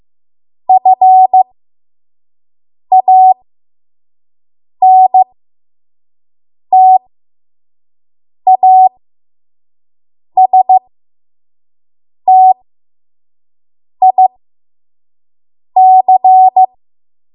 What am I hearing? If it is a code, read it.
Morse code, heard as a keyed tone, about 15 wpm, spaced as for 5 wpm, 750 Hz. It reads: FANTASTIC
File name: Morse / 05WPM